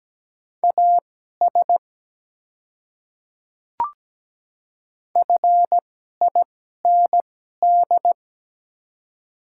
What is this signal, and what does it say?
Morse code, heard as a keyed tone, about 17 wpm, spaced as for 12 wpm, 690 Hz. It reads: AS FIND
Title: 2x Character Spacing - Top 100 Words - 17wpm